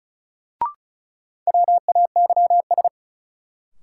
Morse code, heard as a keyed tone, about 36 wpm, 700 Hz. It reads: WAYS
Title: Sets of 5 Words Encore - New Word Drill - 35wpm